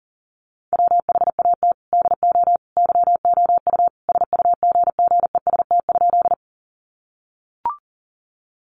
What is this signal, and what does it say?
Morse code, heard as a keyed tone, about 40 wpm, 700 Hz. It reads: WHAT DO YOU SUGGEST?
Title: Sentences from Top 1300 Words 40wpm